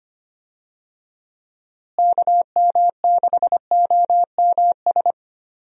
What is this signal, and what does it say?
Morse code, heard as a keyed tone, about 25 wpm, 700 Hz. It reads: KM6OMS
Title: USA Call Signs - 6 Characters - Repeated 2x - 25wpm